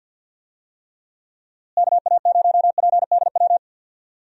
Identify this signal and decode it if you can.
Morse code, heard as a keyed tone, about 51 wpm, 700 Hz. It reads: KA0PDW
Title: USA Call Signs - 6 Characters - Repeated 3x - 50wpm